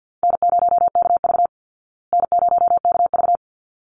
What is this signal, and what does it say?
Morse code, heard as a keyed tone, about 50 wpm, 700 Hz. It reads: N0KV N0KV
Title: Call Signs Repeated 3x - 50wpm